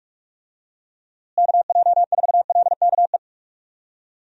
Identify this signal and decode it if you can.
Morse code, heard as a keyed tone, about 41 wpm, 700 Hz. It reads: KJ4LKE
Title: USA Call Signs - 6 Characters - Repeated 2x - 45wpm